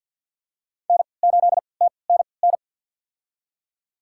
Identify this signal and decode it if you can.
Morse code, heard as a keyed tone, about 49 wpm, spaced as for 28 wpm, 700 Hz. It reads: N8TNN
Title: USA Call Signs - 5 Characters - 3x Character Spacing - 50wpm